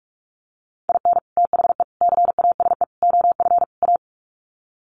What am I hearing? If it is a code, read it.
Morse code, heard as a keyed tone, about 44 wpm, 700 Hz. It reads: IN THE CASE OF A
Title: Sets of 5 Words Encore 45wpm